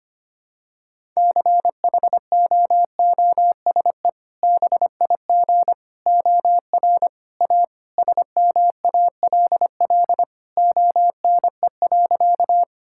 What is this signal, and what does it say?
Morse code, heard as a keyed tone, about 25 wpm, 700 Hz. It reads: CHOOSE BIG OR A SMALL ONE.
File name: practice / 25wpm / Sentences with 2-Word Phrases - 25wpm